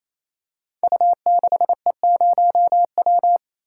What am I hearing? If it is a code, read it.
Morse code, heard as a keyed tone, about 28 wpm, 700 Hz. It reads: U6E0W
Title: Code Groups - 5 Letter:Numbers - 28wpm